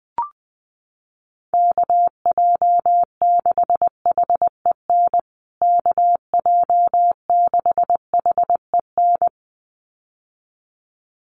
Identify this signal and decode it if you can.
Morse code, heard as a keyed tone, about 20 wpm, 700 Hz. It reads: KJ6HEN KJ6HEN
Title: USA Call Signs Repeated 2x 20wpm